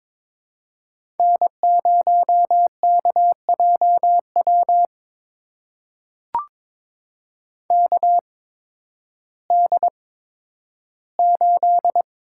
Morse code, heard as a keyed tone, about 22 wpm, spaced as for 12 wpm, 700 Hz. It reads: N0KJW KD8
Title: USA Call Signs - 8x Character Spacing - 22wpm